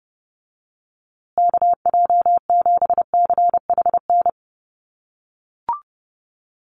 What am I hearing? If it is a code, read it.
Morse code, heard as a keyed tone, about 28 wpm, 700 Hz. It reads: KJ7CHN
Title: USA Call Signs Repeated 3x 30wpm